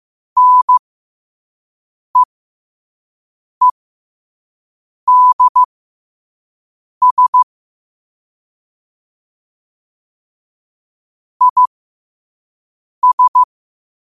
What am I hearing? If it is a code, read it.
Morse code, heard as a keyed tone, about 15 wpm, spaced as for 5 wpm, 1000 Hz. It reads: NEEDS IS
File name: qotdfiles / 5wpm